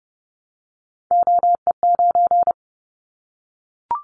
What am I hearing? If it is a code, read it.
Morse code, heard as a keyed tone, about 30 wpm, 700 Hz. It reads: OE9
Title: Callsign Prefixes for Top 25 DXCC - 30wpm